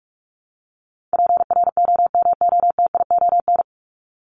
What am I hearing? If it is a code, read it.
Morse code, heard as a keyed tone, about 44 wpm, 700 Hz. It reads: PROMOTION
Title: Top 1501-1750 Words 45wpm